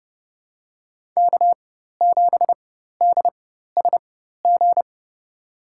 Morse code, heard as a keyed tone, about 32 wpm, spaced as for 14 wpm, 700 Hz. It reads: K7DSG
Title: USA Call Signs - 5 Characters - 4x Character Spacing - 30wpm